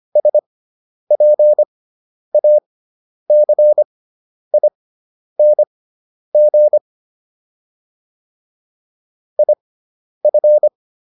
Morse code, heard as a keyed tone, about 25 wpm, spaced as for 10 wpm, 600 Hz. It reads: SPACING IF